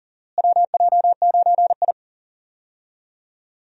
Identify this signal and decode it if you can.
Morse code, heard as a keyed tone, about 40 wpm, 700 Hz. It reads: WJ9I
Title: USA Call Signs - 4 Characters - Repeated 3x - 40wpm